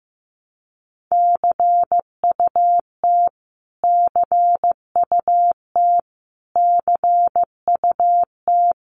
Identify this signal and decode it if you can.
Morse code, heard as a keyed tone, about 15 wpm, 700 Hz. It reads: CUT CUT CUT
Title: Top 500 Words Repeated 15wpm